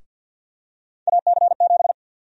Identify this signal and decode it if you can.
Morse code, heard as a keyed tone, about 47 wpm, 700 Hz. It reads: AC7
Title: USA Call Sign Prefixes - 50wpm